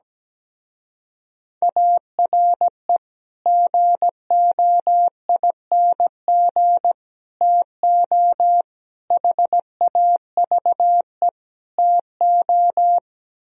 Morse code, heard as a keyed tone, about 17 wpm, 700 Hz. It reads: ARE GOING TO HAVE TO
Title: Speed-Racing - Sets of 5 Words 17wpm